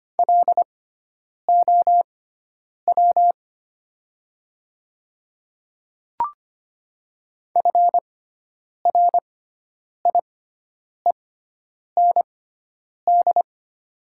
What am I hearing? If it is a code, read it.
Morse code, heard as a keyed tone, about 25 wpm, spaced as for 9 wpm, 700 Hz. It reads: LOW FRIEND
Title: 6x Character Spacing - Top 500 Words - 25wpm